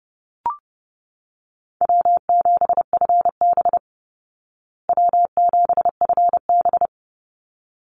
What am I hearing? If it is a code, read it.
Morse code, heard as a keyed tone, about 28 wpm, 700 Hz. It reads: W7FB W7FB
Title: Call Signs Repeated 3x - 30wpm